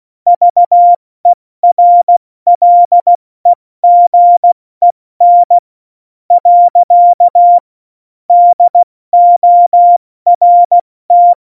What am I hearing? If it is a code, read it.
Morse code, heard as a keyed tone, about 16 wpm, 700 Hz. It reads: VERLEGEN . DORT
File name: Morsekurs / Witze / Witz 02. Januar 2018, 70 F80